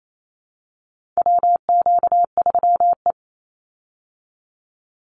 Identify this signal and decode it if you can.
Morse code, heard as a keyed tone, about 28 wpm, 700 Hz. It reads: WQ3E
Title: Call Signs Repeated 2x - 28wpm